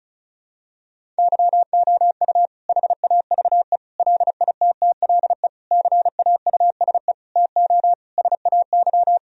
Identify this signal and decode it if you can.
Morse code, heard as a keyed tone, about 36 wpm, 700 Hz. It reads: YOU HAVE LITTLE CAUSE TO SAY
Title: Sentences from Top 1000 Words - Review Part 1 - 35wpm